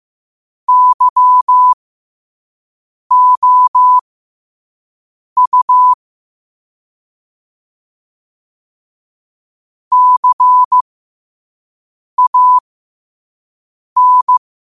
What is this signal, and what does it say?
Morse code, heard as a keyed tone, about 15 wpm, spaced as for 5 wpm, 1000 Hz. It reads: YOU CAN
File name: qotdfiles / 5wpm